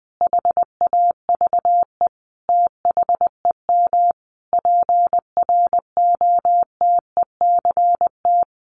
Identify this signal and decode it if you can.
Morse code, heard as a keyed tone, about 20 wpm, 700 Hz. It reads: HAVE THEM PROTECT
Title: Sentences from Top 900 Words 20wpm